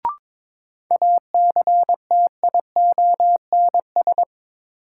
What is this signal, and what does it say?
Morse code, heard as a keyed tone, about 22 wpm, 700 Hz. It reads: ACTIONS
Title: Sentences from Top 1000 Words - Review Part 6 - New Word Drill - 22wpm